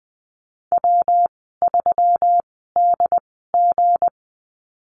Morse code, heard as a keyed tone, about 20 wpm, spaced as for 15 wpm, 700 Hz. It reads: W3DG
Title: 173 Lesson 28g - Number 9 - Call Signs 20wpm